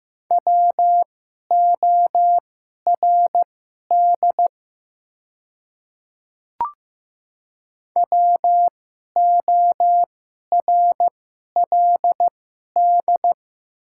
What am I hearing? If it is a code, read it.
Morse code, heard as a keyed tone, about 15 wpm, spaced as for 11 wpm, 700 Hz. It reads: WORD WORLD
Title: 2x Character Spacing - Top 200 Words - 15wpm